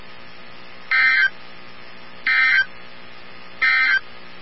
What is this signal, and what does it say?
Emergency Alert System (SAME) tones, heard as harsh buzzing bursts. It is heard from 0.8 to 4.1 s.
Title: NOAA Weather Radio 162.550